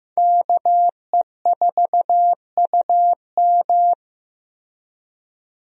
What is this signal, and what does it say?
Morse code, heard as a keyed tone, about 15 wpm, 700 Hz. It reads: KE4UM
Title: USA Call Signs - 5 Characters - Repeated 2x - 15wpm